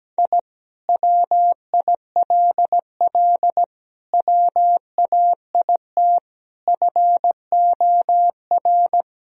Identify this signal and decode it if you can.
Morse code, heard as a keyed tone, about 17 wpm, 700 Hz. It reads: I WILL WAIT FOR
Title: Sentences from Top 1000 Words - Review Part 1 - 17wpm